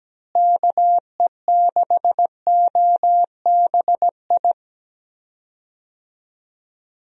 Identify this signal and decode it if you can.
Morse code, heard as a keyed tone, about 17 wpm, 700 Hz. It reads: KE6OBI
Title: USA Call Signs - 6 Characters - Repeated 3x - 17wpm